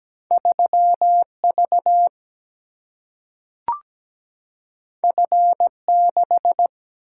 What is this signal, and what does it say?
Morse code, heard as a keyed tone, about 17 wpm, 700 Hz. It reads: 3V F6
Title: Single-Letter Single-Number 17wpm